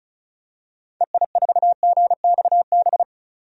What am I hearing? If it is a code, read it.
Morse code, heard as a keyed tone, about 34 wpm, 700 Hz. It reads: EI4GXB
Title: DXCC-100 Call Signs - 6 Characters - 35wpm